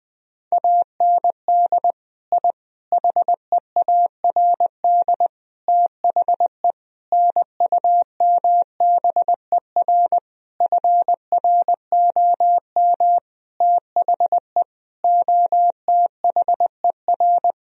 Morse code, heard as a keyed tone, about 20 wpm, 700 Hz. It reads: AND I HEARD THE NUMBER FROM THE OTHER
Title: Sentences from Top 1000 Words - Review Part 1 - 20wpm